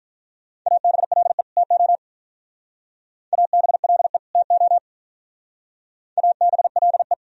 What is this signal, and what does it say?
Morse code, heard as a keyed tone, about 44 wpm, 700 Hz. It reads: ABLE TO ABLE TO ABLE